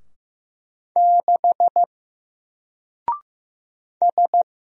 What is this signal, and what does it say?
Morse code, heard as a keyed tone, about 15 wpm, 700 Hz. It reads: 6 S
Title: Single Letter-Number Alt 15wpm